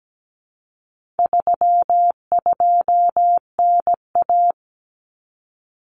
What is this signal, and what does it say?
Morse code, heard as a keyed tone, about 17 wpm, 700 Hz. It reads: 32NA
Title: Common Cut Numbers - 4-Digit 17wpm